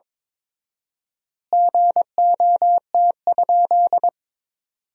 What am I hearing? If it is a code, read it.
Morse code, heard as a keyed tone, about 22 wpm, 700 Hz. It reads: GOT?
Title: 4x Word-Spacing - Sentences from Top 300 Words - 22wpm